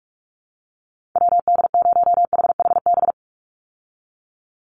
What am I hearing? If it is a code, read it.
Morse code, heard as a keyed tone, about 44 wpm, 700 Hz. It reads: WD0HHB
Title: USA Call Signs 45wpm